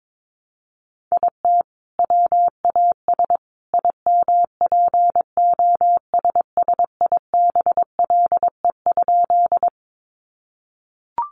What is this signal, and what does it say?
Morse code, heard as a keyed tone, about 22 wpm, 700 Hz. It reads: IT WAS IMPOSSIBLE?
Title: Sentences from Top 1100 Words 22wpm